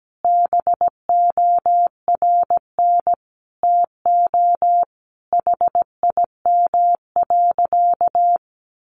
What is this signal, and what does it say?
Morse code, heard as a keyed tone, about 17 wpm, 700 Hz. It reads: BORN TO HIM.
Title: Sentences from Top 1500 Words 17wpm